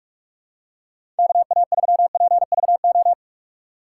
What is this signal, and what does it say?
Morse code, heard as a keyed tone, about 46 wpm, 700 Hz. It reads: KA3PVO
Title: USA Call Signs - 6 Characters - 4x Character Spacing - 45wpm